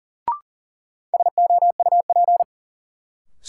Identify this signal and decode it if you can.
Morse code, heard as a keyed tone, about 40 wpm, 700 Hz. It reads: SOUP
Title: Sentences from Top 1600 Words - New Word Drill Alt - 40wpm